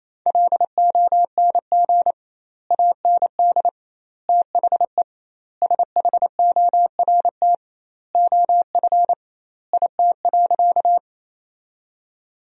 Morse code, heard as a keyed tone, about 28 wpm, 700 Hz. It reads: LONG AND THE SHORT OF IT.
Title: 2x Word-Spacing - Sentences from Top 400 Words - 28wpm